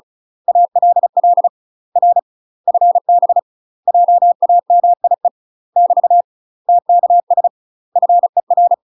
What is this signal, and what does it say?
Morse code, heard as a keyed tone, about 36 wpm, 700 Hz. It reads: ALL R FB JAMIE <BT> TKS FER
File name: UK MASTER QSOs TRAINING / 35WPM / UK MASTER QSOs 0010